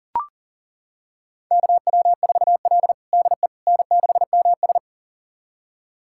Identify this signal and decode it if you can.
Morse code, heard as a keyed tone, about 40 wpm, 700 Hz. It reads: KW4L DE N6MS
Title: USA Callsign DE Callsign - 4 Characters - 40wpm